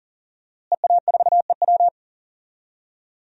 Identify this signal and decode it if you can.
Morse code, heard as a keyed tone, about 40 wpm, 700 Hz. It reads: EA4EW
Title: DXCC-100 Call Signs - 5 Characters - Repeated 2x - 40wpm